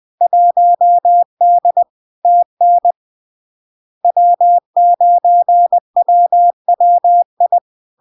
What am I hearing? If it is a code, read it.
Morse code, heard as a keyed tone, about 20 wpm, 700 Hz. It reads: 1D TN W9WWI
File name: LARK-FD-2022-Log-in-CW-20wpm-700-Hz